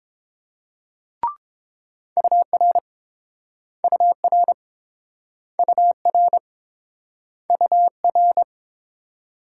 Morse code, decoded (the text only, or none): UR UR UR UR